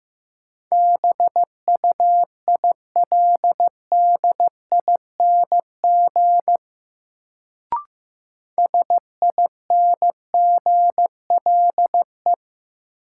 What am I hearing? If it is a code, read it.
Morse code, heard as a keyed tone, about 15 wpm, 700 Hz. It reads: BUILDING SINGLE
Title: Sentences from Top 700 Words - New Word Drill - 15wpm